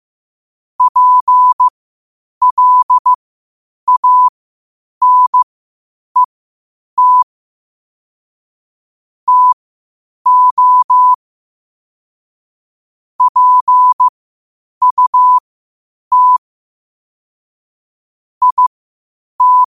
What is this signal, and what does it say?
Morse code, heard as a keyed tone, about 15 wpm, spaced as for 8 wpm, 1000 Hz. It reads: PLANET TO PUT IT
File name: qotdfiles / 8wpm